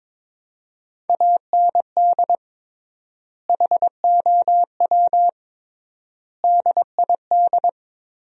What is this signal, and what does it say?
Morse code, heard as a keyed tone, about 22 wpm, 700 Hz. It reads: AND HOW DID
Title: 3x Word-Spacing - Sentences from Top 200 Words - 22wpm